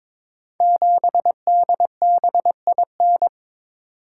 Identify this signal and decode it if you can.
Morse code, heard as a keyed tone, about 22 wpm, 700 Hz. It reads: 7DBIN
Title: Code Groups - 5 Letter:Numbers - 22wpm